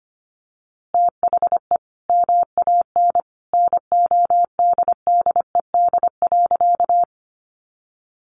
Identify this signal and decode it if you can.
Morse code, heard as a keyed tone, about 25 wpm, 700 Hz. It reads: THE MAN NODDED.
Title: A Sound Of Thunder 25wpm